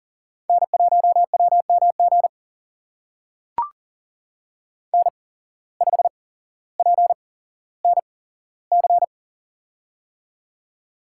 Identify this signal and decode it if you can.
Morse code, heard as a keyed tone, about 42 wpm, spaced as for 17 wpm, 700 Hz. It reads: N1WMG N5PNC
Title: USA Call Signs - 5 Characters - 8x Character Spacing - 40wpm